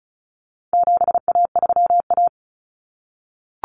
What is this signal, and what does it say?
Morse code, heard as a keyed tone, about 34 wpm, 700 Hz. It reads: 7A3A
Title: Common Cut Numbers - 4-Digit 35wpm